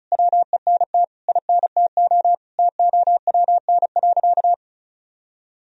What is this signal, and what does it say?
Morse code, heard as a keyed tone, about 36 wpm, 700 Hz. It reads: WENT INTO TOWN.
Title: Sentences from Top 1000 Words - Review Part 1 - 35wpm